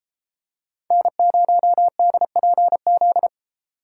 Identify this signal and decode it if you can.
Morse code, heard as a keyed tone, about 32 wpm, 700 Hz. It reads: N0DPZ